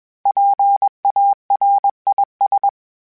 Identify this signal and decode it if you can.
Morse code, heard as a keyed tone, about 21 wpm, 800 Hz. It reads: PARIS
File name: PARIS Calibration / PARIS mp3s / PARIS@20